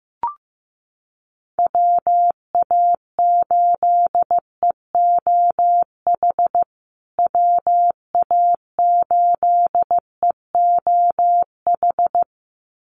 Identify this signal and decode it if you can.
Morse code, heard as a keyed tone, about 15 wpm, 700 Hz. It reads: WA8EOH WA8EOH
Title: USA Call Signs Repeated 2x 15wpm